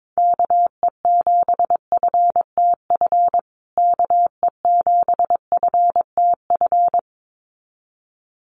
Morse code, heard as a keyed tone, about 22 wpm, 700 Hz. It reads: KE7FTF KE7FTF
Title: USA Call Signs Repeated 2x 22wpm